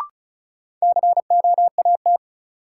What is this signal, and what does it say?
Morse code, heard as a keyed tone, about 36 wpm, 700 Hz. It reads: COAT
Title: Sentences from Top 1600 Words - New Word Drill Alt - 35wpm